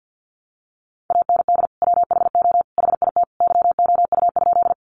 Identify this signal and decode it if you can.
Morse code, heard as a keyed tone, about 48 wpm, 700 Hz. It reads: AND WHO HIT YOU?
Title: Sentences from Top 1200 Words 50wpm